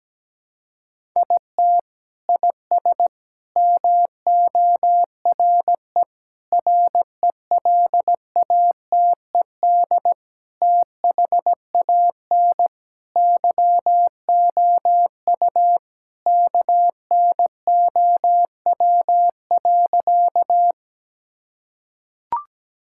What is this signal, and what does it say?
Morse code, heard as a keyed tone, about 17 wpm, 700 Hz. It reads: IT IS MORE RELATED THAN YOU KNOW.
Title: Sentences from Top 1000 Words - Review Part 5 - 17wpm